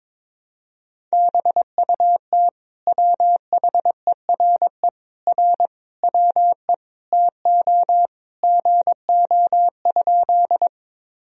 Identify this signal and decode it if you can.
Morse code, heard as a keyed tone, about 21 wpm, 700 Hz. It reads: BUT WHERE R WE TO GO?
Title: Sentences from Top 200 Words and Abbreviations 22wpm